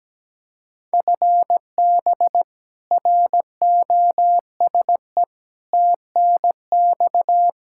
Morse code, heard as a keyed tone, about 17 wpm, 700 Hz. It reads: FB ROSE TNX
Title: Practice QSO - Exchange 3 - More Info - 17wpm